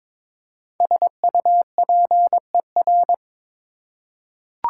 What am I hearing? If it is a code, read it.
Morse code, heard as a keyed tone, about 22 wpm, 700 Hz. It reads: SUPER
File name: 5 Letter Words - First 3 Letters Given - 22wpm